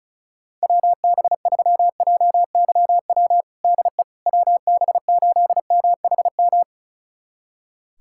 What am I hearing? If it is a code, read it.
Morse code, heard as a keyed tone, about 34 wpm, 700 Hz. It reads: WB3JYW DE WB8MHM
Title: USA Callsign DE Callsign - 6 Characters - 35wpm